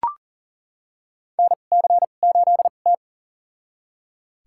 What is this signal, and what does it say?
Morse code, heard as a keyed tone, about 37 wpm, spaced as for 28 wpm, 700 Hz. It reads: NC8T
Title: USA Call Signs - 4 Characters - 2x Character Spacing - 40wpm